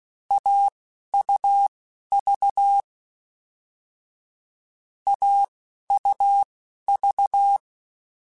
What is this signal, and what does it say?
Morse code, heard as a keyed tone, about 16 wpm, spaced as for 11 wpm, 800 Hz. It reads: AUV AUV